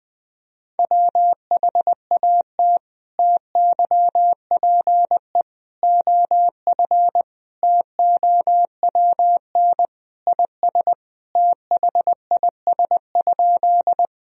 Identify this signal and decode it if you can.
Morse code, heard as a keyed tone, about 19 wpm, 700 Hz. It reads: WHAT TYPE OF TOWN IS THIS?
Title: Sentences from Top 1000 Words - Review Part 3 - 20wpm